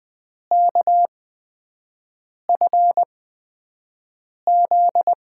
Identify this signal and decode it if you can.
Morse code, heard as a keyed tone, about 20 wpm, spaced as for 5 wpm, 700 Hz. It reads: KFZ
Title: USA Call Signs - 6 Characters - 8x Character Spacing - 20wpm